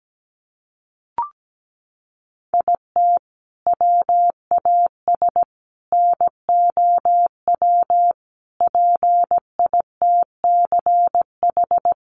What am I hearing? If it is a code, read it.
Morse code, heard as a keyed tone, about 17 wpm, 700 Hz. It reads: IT WAS NOW PITCH
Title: Sentences from Top 1300 Words 17wpm